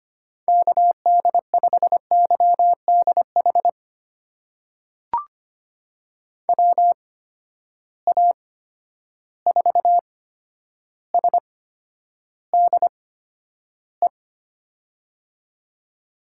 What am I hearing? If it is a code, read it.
Morse code, heard as a keyed tone, about 25 wpm, spaced as for 11 wpm, 700 Hz. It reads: KD5YDH WA4SDE
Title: USA Call Signs - 8x Character Spacing - 25wpm